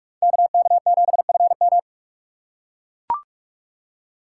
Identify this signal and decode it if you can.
Morse code, heard as a keyed tone, about 43 wpm, 700 Hz. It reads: KK7FM
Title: USA Call Signs - 4x Character Spacing - 45wpm